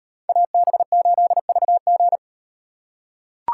Morse code, heard as a keyed tone, about 36 wpm, 700 Hz. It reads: AB8VG